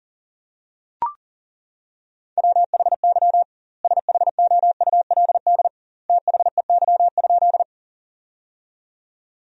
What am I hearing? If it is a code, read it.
Morse code, heard as a keyed tone, about 40 wpm, 700 Hz. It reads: WHY SHOULD THEY?
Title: 2x Word-Spacing - Sentences from Top 300 Words - 40wpm